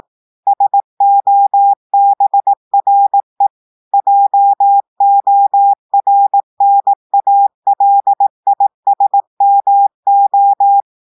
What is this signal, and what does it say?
Morse code, heard as a keyed tone, about 18 wpm, 800 Hz. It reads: SOBRE JORNALISMO
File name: audios / portugais / portugais texte en clair 1 vitesse 18 mots minutes